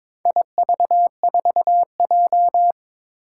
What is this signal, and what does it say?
Morse code, heard as a keyed tone, about 22 wpm, 700 Hz. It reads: IV4J
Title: Four Challenging Letter-Number Combinations 22wpm